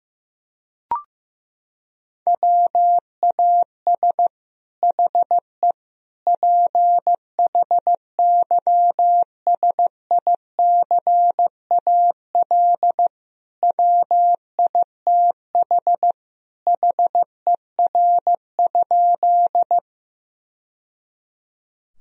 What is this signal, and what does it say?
Morse code, heard as a keyed tone, about 16 wpm, 700 Hz. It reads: WAS HE PHYSICAL WITH HER?
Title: Sentences from Top 1000 Words - Review Part 4 - 15wpm